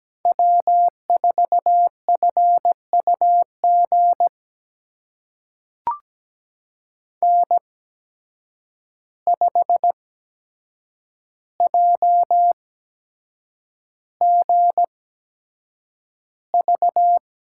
Morse code, heard as a keyed tone, about 17 wpm, spaced as for 7 wpm, 700 Hz. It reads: W4FUG N5JGV
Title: USA Call Signs - 5 Characters - 8x Character Spacing - 17wpm